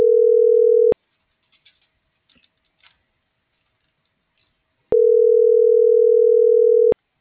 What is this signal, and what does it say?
Ringback tone (North America), 440 + 480 Hz.